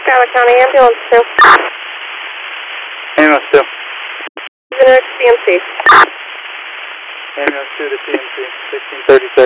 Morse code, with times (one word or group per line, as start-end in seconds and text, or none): none